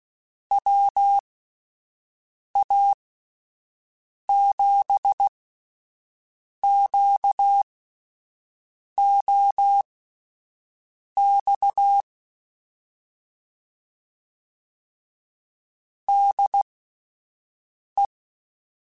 Morse code, heard as a keyed tone, about 16 wpm, spaced as for 5 wpm, 800 Hz. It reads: WA7QOX DE
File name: (spaced) / QSOs Spaced-mp3s / WA7QOX 1 @10 sp